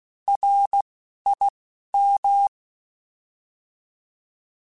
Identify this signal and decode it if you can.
Morse code, heard as a keyed tone, about 16 wpm, 800 Hz. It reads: R I M